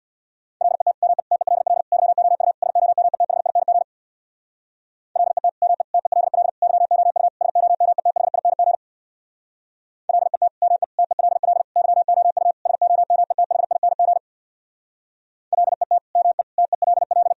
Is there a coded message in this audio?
Yes: Morse code from 0.6 to 17.3 s, heard as a keyed tone, about 64 wpm, 700 Hz. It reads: LET ME TELL YOU SOMETHING LET ME TELL YOU SOMETHING LET ME TELL YOU SOMETHING LET ME TELL